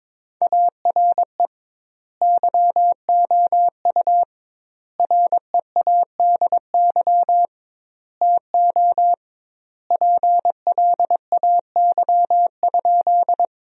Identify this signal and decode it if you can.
Morse code, heard as a keyed tone, about 22 wpm, 700 Hz. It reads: ARE YOU READY TO PLAY?
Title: 2x Word-Spacing - Sentences from Top 400 Words - 22wpm